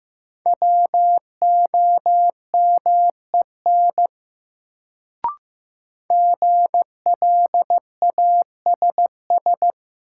Morse code, heard as a keyed tone, about 15 wpm, 700 Hz. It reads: WOMEN GLASS
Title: Sentences from Top 1300 Words - New Word Drill Alt - 15wpm